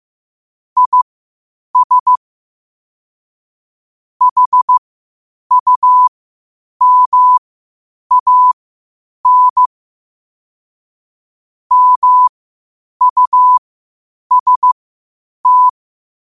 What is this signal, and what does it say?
Morse code, heard as a keyed tone, about 15 wpm, spaced as for 8 wpm, 1000 Hz. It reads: IS HUMAN MUST
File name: qotdfiles / 8wpm